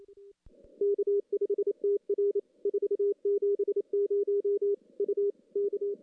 Morse code, heard as a keyed tone, about 30 wpm, 410 Hz. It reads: K5TR 470 U K
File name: CW / CW-SS-k5tr- -1700